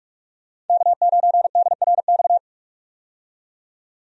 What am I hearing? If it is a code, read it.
Morse code, heard as a keyed tone, about 46 wpm, 700 Hz. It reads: K9DRX
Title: USA Call Signs - 5 Characters - 45wpm